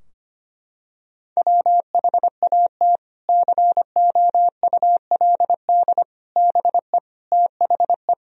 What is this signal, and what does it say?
Morse code, heard as a keyed tone, about 25 wpm, 700 Hz. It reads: WHAT COULD BE THE
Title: Sentences from Top 1000 Words - Review Part 3 - 25wpm